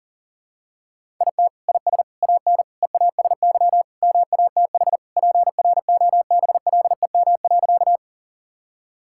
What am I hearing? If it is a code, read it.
Morse code, heard as a keyed tone, about 40 wpm, 700 Hz. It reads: IT IS AN EASY MATH PROBLEM.
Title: Sentences from Top 1000 Words - Review Part 6 - 40wpm